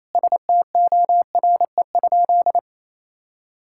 Morse code, heard as a keyed tone, about 28 wpm, 700 Hz. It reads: STORE?
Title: Sentences from Top 1000 Words - Review Part 6 - 28wpm